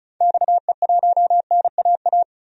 Morse code, heard as a keyed tone, about 34 wpm, 700 Hz. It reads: XE1NAA
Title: DXCC-100 Call Signs - 6 Characters - 35wpm